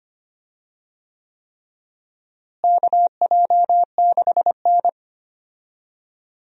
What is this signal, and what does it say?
Morse code, heard as a keyed tone, about 25 wpm, 700 Hz. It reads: KJ6N
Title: USA Call Signs - 4 Characters - Repeated 3x - 25wpm